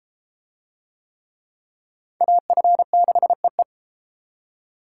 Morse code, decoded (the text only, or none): AF6EE